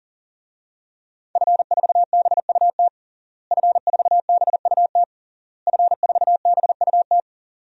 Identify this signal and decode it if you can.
Morse code, heard as a keyed tone, about 40 wpm, 700 Hz. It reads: F4BUT F4BUT F4BUT
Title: DXCC-100 Call Signs - 5 Characters - Repeated 3x - 40wpm